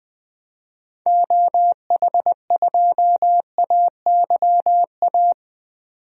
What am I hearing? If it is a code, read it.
Morse code, heard as a keyed tone, about 20 wpm, 700 Hz. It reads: OH2AYA
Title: DXCC-100 Call Signs - 6 Characters - 20wpm